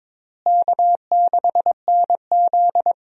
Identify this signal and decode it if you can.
Morse code, heard as a keyed tone, about 22 wpm, 700 Hz. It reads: K6NZ
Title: USA Call Signs - 4 Characters - Repeated 3x - 22wpm